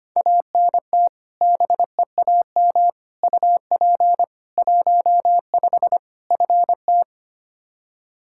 Morse code, heard as a keyed tone, about 25 wpm, 700 Hz. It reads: ANT BEAM UP 15 FT
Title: Practice QSO - Exchange 2 - Info - 25wpm